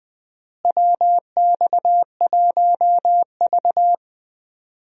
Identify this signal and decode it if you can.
Morse code, heard as a keyed tone, about 20 wpm, 700 Hz. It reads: WX1V
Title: USA Call Signs - 4 Characters - Repeated 3x - 20wpm